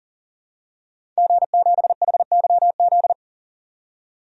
Morse code, heard as a keyed tone, about 40 wpm, 700 Hz. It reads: G7HYZ
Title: DXCC-100 Call Signs - 5 Characters - 40wpm